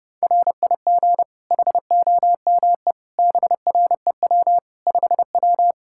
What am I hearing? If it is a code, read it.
Morse code, heard as a keyed tone, about 30 wpm, 700 Hz. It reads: RIG HOME BREW 5W
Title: Practice QSO - Exchange 2 - Info - 30wpm